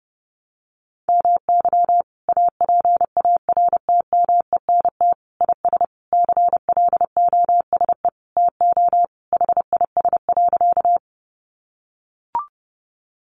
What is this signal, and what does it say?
Morse code, heard as a keyed tone, about 30 wpm, 700 Hz. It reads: MY APARTMENT IS CLOSE TO HIS.
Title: Sentences from Top 900 Words 30wpm